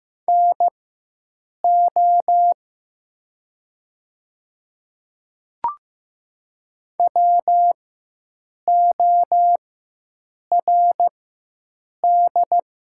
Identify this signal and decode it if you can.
Morse code, heard as a keyed tone, about 15 wpm, spaced as for 7 wpm, 700 Hz. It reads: NO WORD
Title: 4x Character Spacing - Top 100 Words - 15wpm